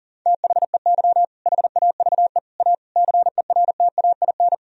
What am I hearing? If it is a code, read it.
Morse code, heard as a keyed tone, about 40 wpm, 700 Hz. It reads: THEY HAVE A CERTAIN
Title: Sentences from Top 1000 Words - Review Part 4 - 40wpm